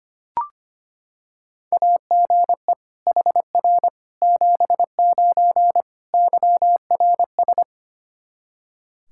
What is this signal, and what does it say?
Morse code, heard as a keyed tone, about 25 wpm, 700 Hz. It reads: AGE HR 79 YRS
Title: Practice QSO - Exchange 3 - More Info - 25wpm